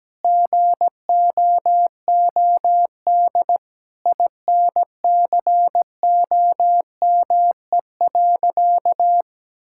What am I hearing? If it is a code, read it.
Morse code, heard as a keyed tone, about 17 wpm, 700 Hz. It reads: GOOD INCOME.
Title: Sentences from Top 1000 Words - Review Part 6 - 17wpm